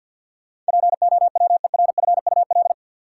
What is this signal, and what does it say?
Morse code, heard as a keyed tone, about 53 wpm, 700 Hz. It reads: POWERFUL
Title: Sentences from Top 1000 Words - Review Part 6 - New Word Drill - 50wpm